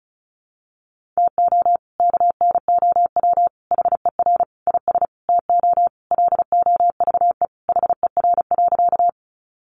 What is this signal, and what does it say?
Morse code, heard as a keyed tone, about 34 wpm, 700 Hz. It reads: TO KNOW HER IS TO LOVE HER.
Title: Sentences from Top 400 Words 35wpm